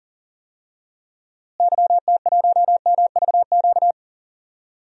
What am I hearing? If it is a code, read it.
Morse code, heard as a keyed tone, about 37 wpm, 700 Hz. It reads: YT1MVQ
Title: DXCC-100 Call Signs - 6 Characters - Repeated 3x - 40wpm